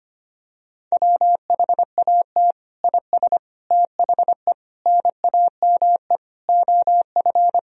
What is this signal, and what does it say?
Morse code, heard as a keyed tone, about 25 wpm, 700 Hz. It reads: WHAT IS THE NAME OF
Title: Sentences from Top 1000 Words - Review Part 2 - 25wpm